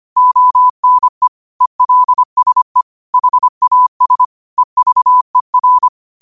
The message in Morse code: ONE ELSE HAS EVER

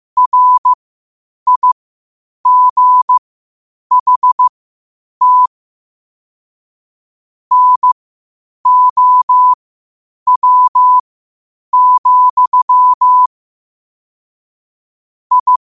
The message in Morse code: RIGHT NOW, I

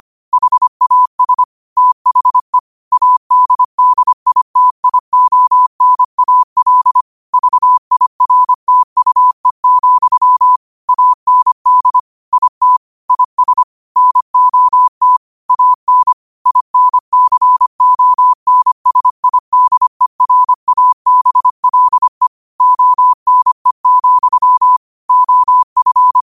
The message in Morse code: HAS THE ADDITIONAL VIRTUE, AND IT IS NOT AN INCONSIDERABLE ONE, OF

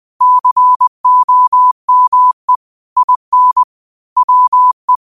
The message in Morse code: COME IN WE

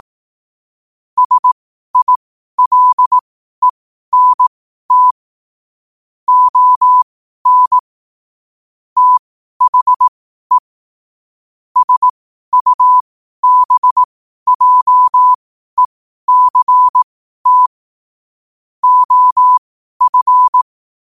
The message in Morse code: SILENT ON THE SUBJECT OF

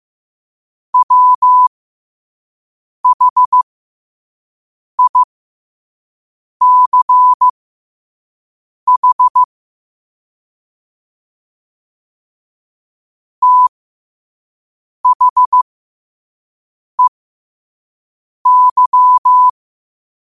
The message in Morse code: WHICH THEY

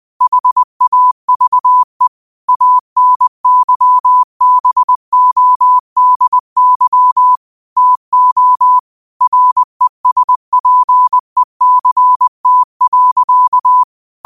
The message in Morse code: HAVE ANYBODY TO RESPECT.

